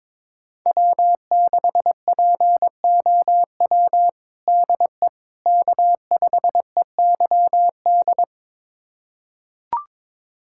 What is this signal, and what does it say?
Morse code, heard as a keyed tone, about 22 wpm, 700 Hz. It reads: W6POW DE K5EYD
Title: USA Callsign DE Callsign 22wpm